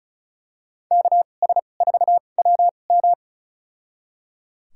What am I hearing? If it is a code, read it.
Morse code, heard as a keyed tone, about 34 wpm, spaced as for 25 wpm, 700 Hz. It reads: KS4WM
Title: USA Call Signs - 2x Character Spacing - 35wpm